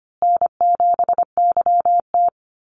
Morse code, heard as a keyed tone, about 25 wpm, 700 Hz. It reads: N7YT
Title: Easy Call Signs Repeated 3x - 25wpm